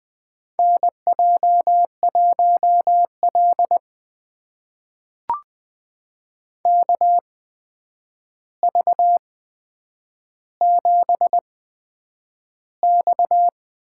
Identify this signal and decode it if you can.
Morse code, heard as a keyed tone, about 20 wpm, spaced as for 9 wpm, 700 Hz. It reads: NJ1L KV7X
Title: USA Call Signs - 4 Characters - 8x Character Spacing - 20wpm